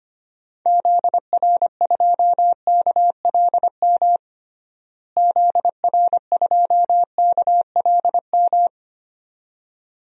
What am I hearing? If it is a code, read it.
Morse code, heard as a keyed tone, about 25 wpm, 700 Hz. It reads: ZR2KLM ZR2KLM
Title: DXCC-100 Call Signs - 6 Characters - Repeated 2x - 25wpm